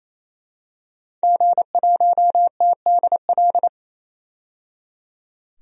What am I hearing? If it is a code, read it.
Morse code, heard as a keyed tone, about 27 wpm, 690 Hz. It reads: G1TDL
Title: DXCC-100 Call Signs - 5 Characters - 28wpm